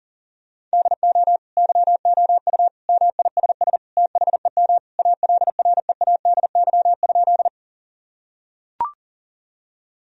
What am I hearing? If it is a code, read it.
Morse code, heard as a keyed tone, about 37 wpm, 700 Hz. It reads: DO YOU MISS THEM ALREADY?
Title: Sentences from Top 1000 Words - Review Part 3 - 40wpm